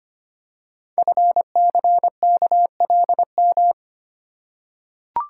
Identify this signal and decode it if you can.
Morse code, heard as a keyed tone, about 25 wpm, 700 Hz. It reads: FCKLM
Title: Code Groups - 5 Letters - 25wpm